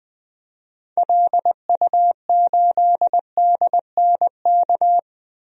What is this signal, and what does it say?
Morse code, heard as a keyed tone, about 21 wpm, 700 Hz. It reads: LU8DNK
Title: DXCC-100 Call Signs - 6 Characters - 20wpm